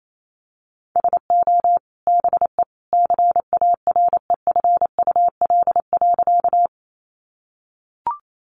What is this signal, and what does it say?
Morse code, heard as a keyed tone, about 28 wpm, 700 Hz. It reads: SO BE CAREFUL.
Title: A Sound Of Thunder 28wpm